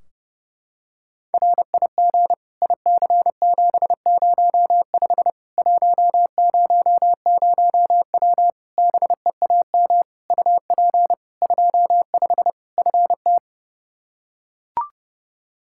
Morse code, heard as a keyed tone, about 30 wpm, 700 Hz. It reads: RIG IC705 100W BEAM UP 25 FT
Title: K1USN SST QSO Practice 30wpm